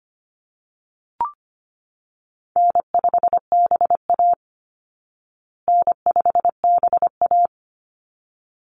Morse code, heard as a keyed tone, about 25 wpm, 700 Hz. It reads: N5BA N5BA
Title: Call Signs Repeated 3x - 25wpm